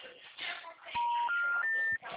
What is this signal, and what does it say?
Special information tone, with noise.